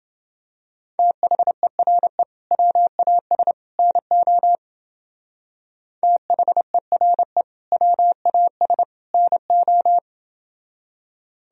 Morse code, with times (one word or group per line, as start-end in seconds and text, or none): 1.0-2.2 THERE
2.5-3.5 WAS
3.8-4.6 NO
6.0-7.4 THERE
7.7-8.8 WAS
9.1-10.0 NO